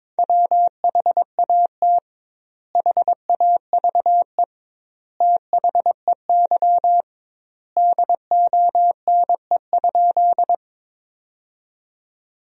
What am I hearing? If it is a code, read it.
Morse code, heard as a keyed tone, about 22 wpm, 700 Hz. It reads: WHAT HAVE THEY DONE?
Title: 2x Word-Spacing - Sentences from Top 500 Words - 22wpm